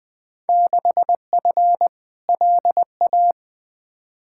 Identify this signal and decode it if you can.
Morse code, heard as a keyed tone, about 20 wpm, 700 Hz. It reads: 6F LA
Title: Field Day Exchanges Encore - 20wpm